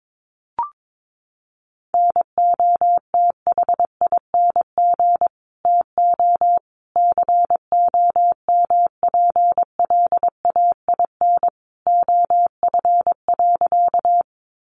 Morse code, heard as a keyed tone, about 22 wpm, 700 Hz. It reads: NOTHING TO COMPLAIN OF.
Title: Sentences from Top 1300 Words 22wpm